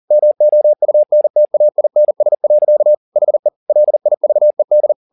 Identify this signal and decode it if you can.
Morse code, heard as a keyed tone, about 40 wpm, 600 Hz. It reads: MOUNTAINS. HE LIVED